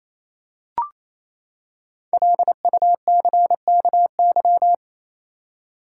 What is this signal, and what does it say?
Morse code, heard as a keyed tone, about 28 wpm, 700 Hz. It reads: LUCKY
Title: Five Letter Words - 28wpm